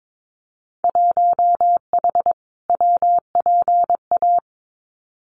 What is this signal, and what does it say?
Morse code, heard as a keyed tone, about 22 wpm, 700 Hz. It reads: 1H WPA
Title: Winter Field Day Exchanges 22wpm